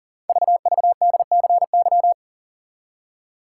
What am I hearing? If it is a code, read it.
Morse code, heard as a keyed tone, about 40 wpm, 700 Hz. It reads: VVDCY
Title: Code Groups - 5 Letters - 40wpm